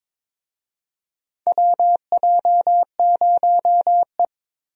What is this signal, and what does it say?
Morse code, heard as a keyed tone, about 22 wpm, 700 Hz. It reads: WJ0E
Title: USA Call Signs - 4 Characters - 22wpm